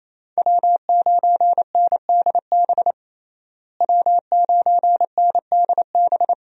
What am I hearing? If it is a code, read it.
Morse code, heard as a keyed tone, about 28 wpm, 700 Hz. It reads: W9NDB W9NDB
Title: USA Call Signs - 5 Characters - Repeated 3x - 28wpm